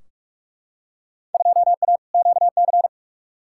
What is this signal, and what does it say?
Morse code, heard as a keyed tone, about 48 wpm, 700 Hz. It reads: 2A QC
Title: Field Day Exchanges - 45wpm